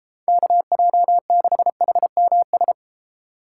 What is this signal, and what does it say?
Morse code, heard as a keyed tone, about 32 wpm, 700 Hz. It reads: KJ6HMS